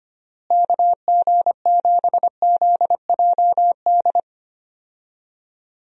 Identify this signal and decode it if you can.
Morse code, heard as a keyed tone, about 25 wpm, 700 Hz. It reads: KG7ZJD
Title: USA Call Signs - 6 Characters - Repeated 3x - 25wpm